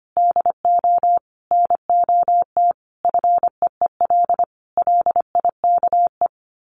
Morse code, heard as a keyed tone, about 25 wpm, 700 Hz. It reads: DO NOT FEEL LIKE
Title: Sets of 5 Words Encore 25wpm